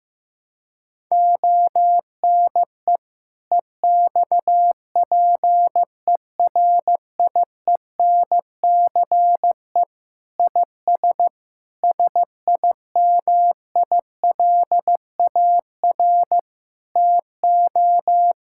Morse code, heard as a keyed tone, about 15 wpm, 700 Hz. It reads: ONE EXPERIENCE IS SIMILAR TO
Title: Sentences from Top 1000 Words - Review Part 5 - 15wpm